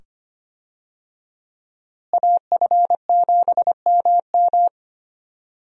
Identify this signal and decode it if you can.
Morse code, heard as a keyed tone, about 25 wpm, 700 Hz. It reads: AF7MM
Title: USA Call Signs - 5 Characters - Repeated 2x - 25wpm